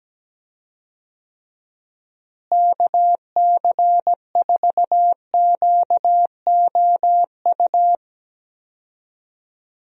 Morse code, heard as a keyed tone, about 17 wpm, 700 Hz. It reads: KC4QOU
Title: USA Call Signs - 6 Characters - Repeated 2x - 17wpm